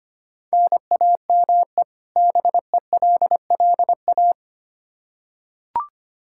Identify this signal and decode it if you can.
Morse code, heard as a keyed tone, about 25 wpm, 700 Hz. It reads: NAME BELLA
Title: Practice QSO - Exchange 1 - Beginning - 25wpm